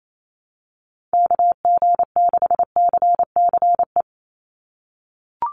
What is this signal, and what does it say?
Morse code, heard as a keyed tone, about 28 wpm, 700 Hz. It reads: KG6CCE
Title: USA Call Signs 28wpm